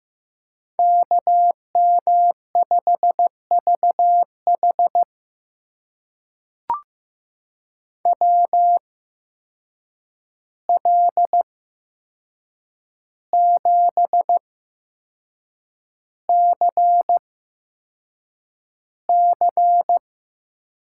Morse code, heard as a keyed tone, about 15 wpm, spaced as for 6 wpm, 700 Hz. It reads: KM5VH WL7CC
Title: USA Call Signs - 5 Characters - 8x Character Spacing - 15wpm